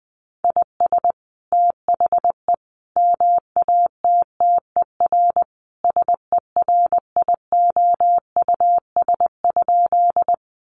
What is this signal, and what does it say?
Morse code, heard as a keyed tone, about 20 wpm, 700 Hz. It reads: IS THE MATTER SERIOUS?
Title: Sentences from Top 1100 Words 20wpm